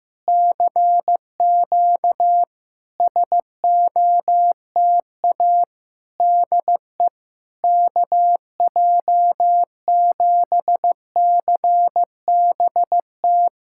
Morse code, heard as a keyed tone, about 15 wpm, 700 Hz. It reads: CQ SOTA DE KJ7CBT
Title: SOTA QSO Practice 15wpm